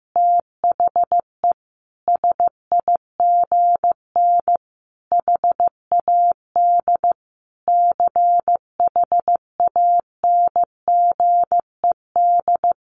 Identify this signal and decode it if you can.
Morse code, heard as a keyed tone, about 15 wpm, 700 Hz. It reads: THE SIGN HAD CHANGED
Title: A Sound Of Thunder 15wpm